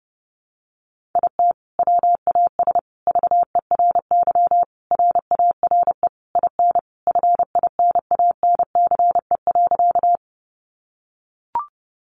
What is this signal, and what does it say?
Morse code, heard as a keyed tone, about 30 wpm, 700 Hz. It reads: IT WAS VERY RARE IN FINANCE.
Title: Sentences from Top 1500 Words 30wpm